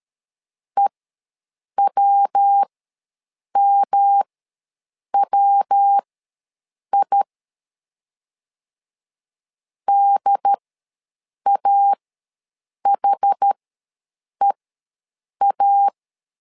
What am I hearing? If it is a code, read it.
Morse code, heard as a keyed tone, about 13 wpm, spaced as for 7 wpm, 800 Hz. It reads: EWMWI DAHEA
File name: LEZIONI / 3 MWRO / 3 MW ws 12wpm 1816